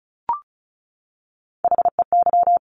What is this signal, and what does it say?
Morse code, heard as a keyed tone, about 36 wpm, 700 Hz. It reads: HEY
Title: Top Three Letter Combinations in Words 35wpm